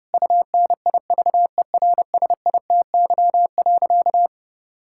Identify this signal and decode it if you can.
Morse code, heard as a keyed tone, about 30 wpm, 700 Hz. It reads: UNIVERSITY.
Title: Sentences from Top 1000 Words - Review Part 6 - 30wpm